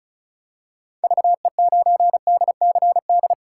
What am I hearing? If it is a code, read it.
Morse code, heard as a keyed tone, about 34 wpm, 700 Hz. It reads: VE9DCD
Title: K1USN SST QSO Practice - 35wpm